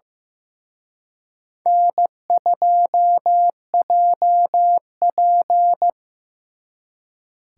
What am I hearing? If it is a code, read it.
Morse code, heard as a keyed tone, about 18 wpm, 700 Hz. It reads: N2JP
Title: Speed Racing - USA 1x2 Call Signs 15wpm